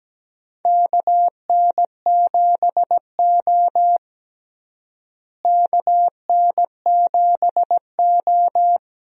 Morse code, heard as a keyed tone, about 17 wpm, 700 Hz. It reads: KN7O KN7O
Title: USA Call Signs - 4 Characters - Repeated 2x - 17wpm